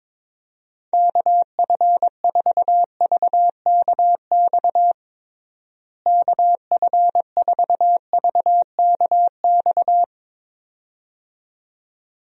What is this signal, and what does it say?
Morse code, heard as a keyed tone, about 22 wpm, 700 Hz. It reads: KF4VKX KF4VKX
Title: USA Call Signs - 6 Characters - Repeated 2x - 22wpm